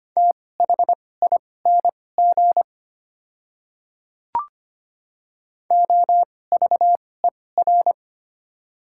Morse code, heard as a keyed tone, about 25 wpm, spaced as for 18 wpm, 700 Hz. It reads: THING OVER
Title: 2x Character Spacing - Top 200 Words - 25wpm